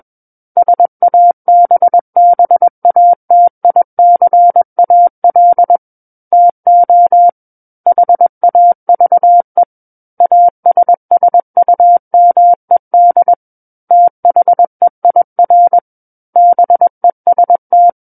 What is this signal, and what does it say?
Morse code, heard as a keyed tone, about 21 wpm, 700 Hz. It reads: SABBATICAL TO HAVE ASSUMED THEIR BEST
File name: Book One / 12-Chapter 12 - 21 WPM